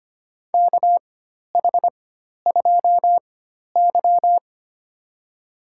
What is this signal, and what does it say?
Morse code, heard as a keyed tone, about 25 wpm, spaced as for 12 wpm, 700 Hz. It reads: KH2Y
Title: USA Call Signs - 4 Characters - 4x Character Spacing - 25wpm